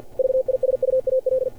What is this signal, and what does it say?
Morse code, heard as a keyed tone, about 49 wpm, 530 Hz. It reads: VISUAL
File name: archived / CRSnet